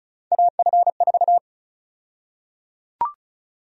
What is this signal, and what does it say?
Morse code, heard as a keyed tone, about 34 wpm, 700 Hz. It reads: AF4
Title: USA Call Sign Prefixes - 35wpm